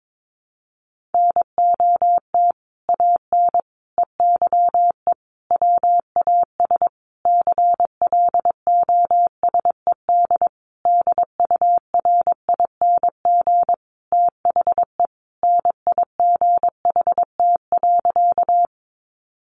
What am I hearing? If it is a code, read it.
Morse code, heard as a keyed tone, about 22 wpm, 700 Hz. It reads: NOT AN EYE WAS CLOSED DURING THE NIGHT.
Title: Sentences from Top 1400 Words 22wpm